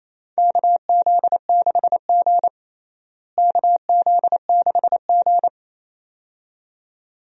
Morse code, heard as a keyed tone, about 28 wpm, 700 Hz. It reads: KZ6G KZ6G
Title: USA Call Signs - 4 Characters - Repeated 2x - 28wpm